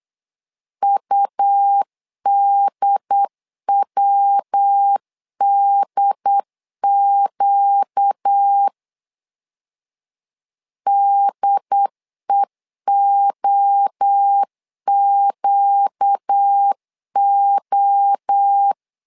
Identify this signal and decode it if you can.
Morse code, heard as a keyed tone, about 8 wpm, 800 Hz. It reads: UDWDQ DEOQO
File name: 4 Q 8wpm